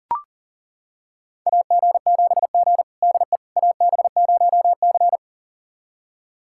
Morse code, heard as a keyed tone, about 40 wpm, 700 Hz. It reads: AG7G DE AB0C
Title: USA Callsign DE Callsign - 4 Characters - 40wpm